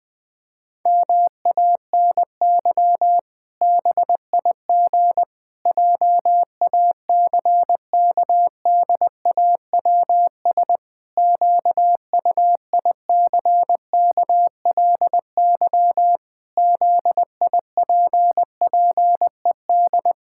Morse code, heard as a keyed tone, about 20 wpm, 700 Hz. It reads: MANY BIG JACKDAWS QUICKLY ZIPPED
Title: Pangram Challenge v2 - 20wpm